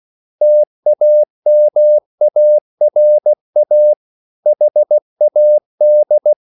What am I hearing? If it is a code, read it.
Morse code, heard as a keyed tone, about 16 wpm, 600 Hz. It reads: TAMARA HAD